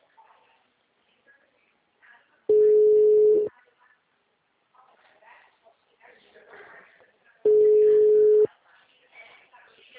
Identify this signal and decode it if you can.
Ringback tone (Europe, 425 Hz), with speech.
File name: out- -2006- - - .1338